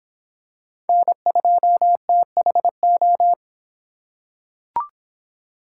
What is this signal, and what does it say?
Morse code, heard as a keyed tone, about 24 wpm, 700 Hz. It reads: N2THO